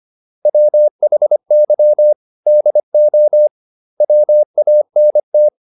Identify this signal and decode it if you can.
Morse code, heard as a keyed tone, about 25 wpm, 600 Hz. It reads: WHY DO WANT